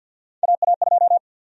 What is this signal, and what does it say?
Morse code, heard as a keyed tone, about 52 wpm, 700 Hz. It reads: AA2
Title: USA Call Sign Prefixes - 50wpm